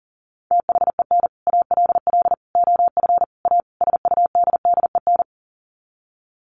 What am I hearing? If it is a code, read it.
Morse code, heard as a keyed tone, about 40 wpm, 700 Hz. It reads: THEN ALL OF A SUDDEN
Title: Sets of 5 Words Encore 40wpm